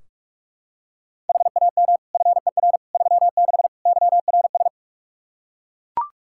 Morse code, heard as a keyed tone, about 41 wpm, 700 Hz. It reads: HAM FER 36 YRS
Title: K1USN SST QSO Practice 45wpm